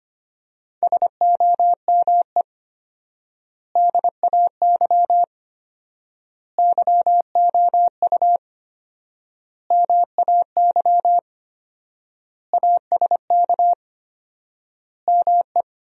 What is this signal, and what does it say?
Morse code, heard as a keyed tone, about 25 wpm, 700 Hz. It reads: SOME DAY YOU MAY ASK ME
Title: 4x Word-Spacing - Sentences from Top 200 Words - 25wpm